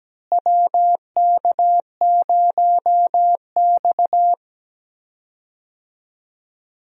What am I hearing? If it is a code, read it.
Morse code, heard as a keyed tone, about 17 wpm, 700 Hz. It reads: WK0X
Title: USA Call Signs - 4 Characters - Repeated 2x - 17wpm